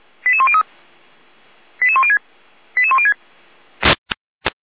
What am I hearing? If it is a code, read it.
Five-tone selective call (ZVEI), heard as quick runs of five tones, tones of about 70 ms. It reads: 88103 88106 88106